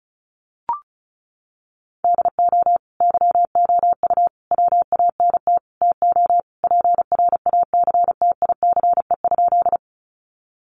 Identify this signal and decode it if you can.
Morse code, heard as a keyed tone, about 34 wpm, 700 Hz. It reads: DO YOU WANT TO PRACTICE?
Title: Sentences from Top 700 Words 35wpm